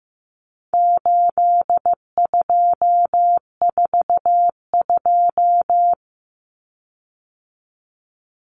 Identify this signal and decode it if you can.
Morse code, heard as a keyed tone, about 15 wpm, 700 Hz. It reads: 8242
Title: Common Cut Numbers - 4-Digit 15wpm